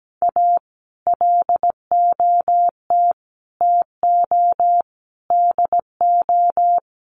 Morse code, heard as a keyed tone, about 17 wpm, 700 Hz. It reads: A LOT TO DO
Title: Sets of 5 Words 17wpm